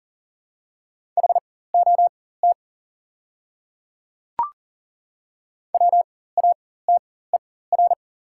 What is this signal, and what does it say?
Morse code, heard as a keyed tone, about 40 wpm, spaced as for 19 wpm, 690 Hz. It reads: HOT WATER
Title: 4x Character Spacing - Top 300 Words - 40wpm